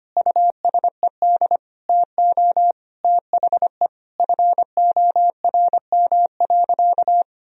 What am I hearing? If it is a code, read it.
Morse code, heard as a keyed tone, about 25 wpm, 700 Hz. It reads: USED TO THE FORM.
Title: Sentences from Top 1000 Words - Review Part 1 - 25wpm